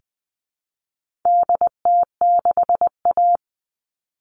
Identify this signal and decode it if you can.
Morse code, heard as a keyed tone, about 20 wpm, 700 Hz. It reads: DT6A
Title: All Cut Numbers - 4-Digit 20wpm